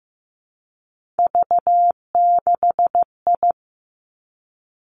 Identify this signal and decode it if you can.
Morse code, heard as a keyed tone, about 15 wpm, 700 Hz. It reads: V6I
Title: Three Challenging Letter-Number Combinations 15wpm